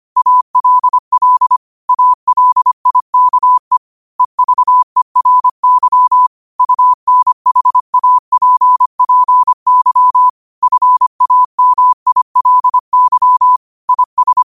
Morse code, heard as a keyed tone, about 25 wpm, 1000 Hz. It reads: ALL ALIKE EVERY UNHAPPY FAMILY IS
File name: qotdfiles / 25wpm